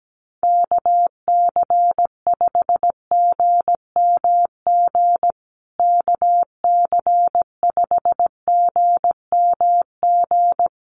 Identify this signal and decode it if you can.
Morse code, heard as a keyed tone, about 17 wpm, 700 Hz. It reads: KC5GMG KC5GMG
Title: USA Call Signs Repeated 3x 17wpm